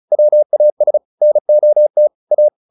Morse code, heard as a keyed tone, about 34 wpm, 600 Hz. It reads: WAS NOT A